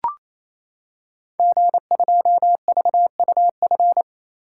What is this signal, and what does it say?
Morse code, heard as a keyed tone, about 29 wpm, 700 Hz. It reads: G2VUF
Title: Code Groups - 5 Letter:Numbers - 28wpm